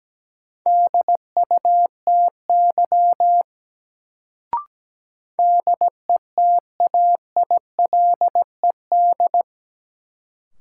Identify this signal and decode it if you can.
Morse code, heard as a keyed tone, about 17 wpm, 700 Hz. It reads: DUTY DETAILED
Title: Sentences from Top 1500 Words - New Word Drill - 17wpm